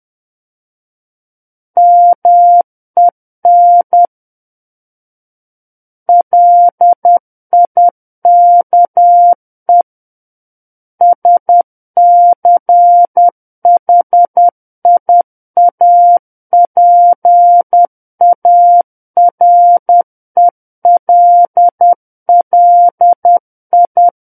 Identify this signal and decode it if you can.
Morse code, heard as a keyed tone, about 10 wpm, 700 Hz. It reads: MEN LIKE SCHIAPARELLI